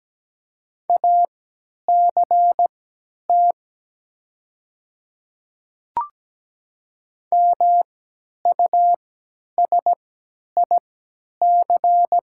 Morse code, heard as a keyed tone, about 17 wpm, spaced as for 10 wpm, 700 Hz. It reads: ACT MUSIC
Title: 3x Character Spacing - Top 300 Words - 17wpm